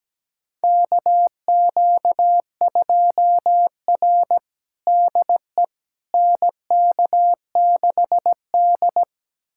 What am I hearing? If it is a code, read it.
Morse code, heard as a keyed tone, about 17 wpm, 700 Hz. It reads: KQ2R DE NK6D
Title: USA Callsign DE Callsign - 4 Characters - 17wpm